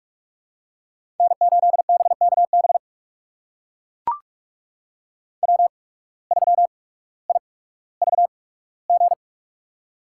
Morse code, heard as a keyed tone, about 42 wpm, spaced as for 19 wpm, 700 Hz. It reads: N8BKB W3IVG
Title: USA Call Signs - 5 Characters - 8x Character Spacing - 45wpm